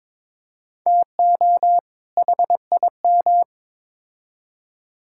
Morse code, heard as a keyed tone, about 21 wpm, 700 Hz. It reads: TO HIM
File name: Build A Phrase - 2-Word Phrases - 22wpm